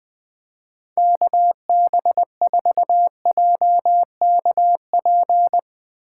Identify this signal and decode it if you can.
Morse code, heard as a keyed tone, about 20 wpm, 700 Hz. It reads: KB4JKP
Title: USA Call Signs - 6 Characters - 2x Character Spacing - 20wpm